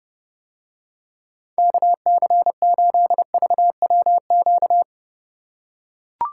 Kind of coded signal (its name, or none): Morse code